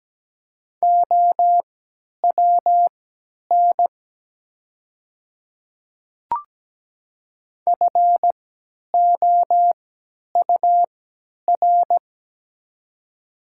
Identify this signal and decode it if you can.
Morse code, heard as a keyed tone, about 17 wpm, spaced as for 10 wpm, 700 Hz. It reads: OWN FOUR
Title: 3x Character Spacing - Top 500 Words - 17wpm